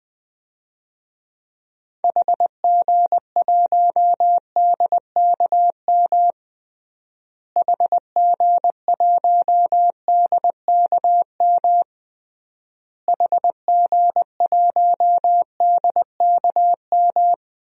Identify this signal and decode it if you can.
Morse code, heard as a keyed tone, about 21 wpm, 700 Hz. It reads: HG1DKM HG1DKM HG1DKM
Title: DXCC-100 Call Signs - 6 Characters - Repeated 3x - 20wpm